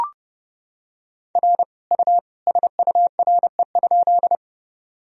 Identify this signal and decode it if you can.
Morse code, heard as a keyed tone, about 32 wpm, 700 Hz. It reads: R U SURE?
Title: Sentences from Top 300 Words and Abbreviations 30wpm